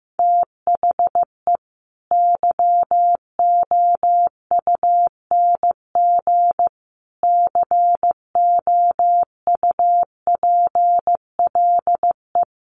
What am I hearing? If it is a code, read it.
Morse code, heard as a keyed tone, about 15 wpm, 700 Hz. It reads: THE YOUNG COUPLE
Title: Sentences from Top 1100 Words 15wpm